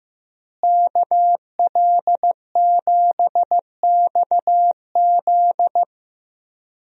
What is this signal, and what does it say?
Morse code, heard as a keyed tone, about 15 wpm, 700 Hz. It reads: KL7XZ
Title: DXCC-100 Call Signs - 5 Characters - Repeated 3x - 15wpm